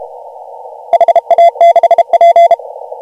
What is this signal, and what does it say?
Morse code, heard as a keyed tone, about 32 wpm, 680 Hz. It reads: HA6P
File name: AUDIO2019 / HG1G CW